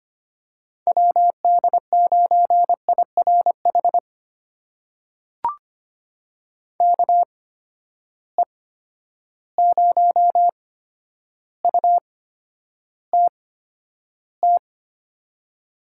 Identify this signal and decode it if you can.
Morse code, heard as a keyed tone, about 25 wpm, spaced as for 11 wpm, 700 Hz. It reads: WD9IRH KE0UTT
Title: USA Call Signs - 6 Characters - 8x Character Spacing - 25wpm